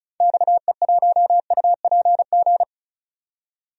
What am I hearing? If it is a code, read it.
Morse code, heard as a keyed tone, about 36 wpm, 700 Hz. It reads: XE1UPG
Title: DXCC-100 Call Signs - 6 Characters - 35wpm